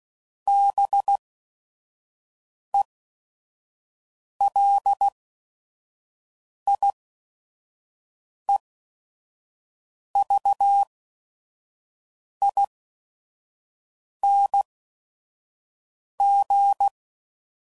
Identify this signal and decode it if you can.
Morse code, heard as a keyed tone, about 16 wpm, spaced as for 5 wpm, 800 Hz. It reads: BELIEVING
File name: Suffixes / Special ING's / VerySlowToFaster ING's